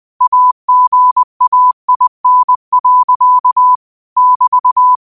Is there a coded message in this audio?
Yes: Morse code from 0.2 to 5.0 s, heard as a keyed tone, about 20 wpm, 1000 Hz. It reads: AGAIN. <BT>